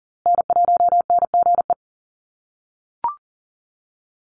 Morse code, heard as a keyed tone, about 40 wpm, 700 Hz. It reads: N1NGE
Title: USA Call Signs Repeated 3x 40wpm